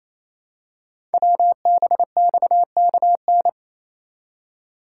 Morse code, heard as a keyed tone, about 28 wpm, 700 Hz. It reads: WBXKN
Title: Code Groups - 5 Letters - 28wpm